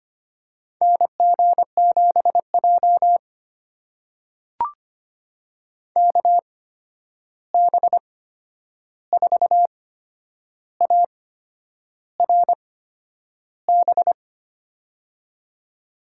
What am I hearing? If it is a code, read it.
Morse code, heard as a keyed tone, about 25 wpm, spaced as for 9 wpm, 700 Hz. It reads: NG7J KB4ARB
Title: USA Call Signs - 8x Character Spacing - 25wpm